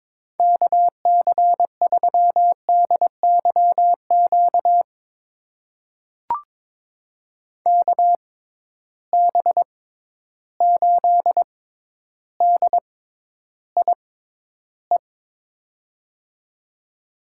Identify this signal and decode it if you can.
Morse code, heard as a keyed tone, about 22 wpm, spaced as for 11 wpm, 700 Hz. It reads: KC3DYQ KB8DIE
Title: USA Call Signs - 6x Character Spacing - 22wpm